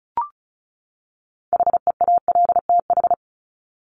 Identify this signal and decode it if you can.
Morse code, heard as a keyed tone, about 34 wpm, 700 Hz. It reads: HEALTH
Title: Top 501-750 Words 35wpm